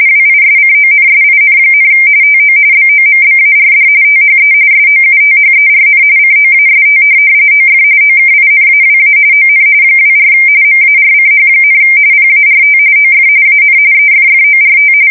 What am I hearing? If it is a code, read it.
RTTY, about 45.45 baud. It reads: WISCONSIN, TO BOMBARD THE SOUTHERN COAST OF OKINAWA ON 24 MARCH, PART OF AN EFFORT
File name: fsk170 / USS-Missouri